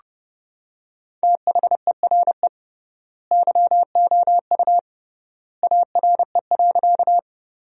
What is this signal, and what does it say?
Morse code, heard as a keyed tone, about 30 wpm, 700 Hz. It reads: THERE YOU ARE.
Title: 3x Word-Spacing - Sentences from Top 100 Words - 30wpm